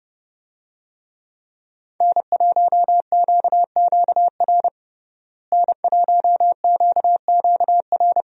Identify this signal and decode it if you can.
Morse code, heard as a keyed tone, about 30 wpm, 700 Hz. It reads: N1QQR N1QQR
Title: USA Call Signs - 5 Characters - Repeated 2x - 30wpm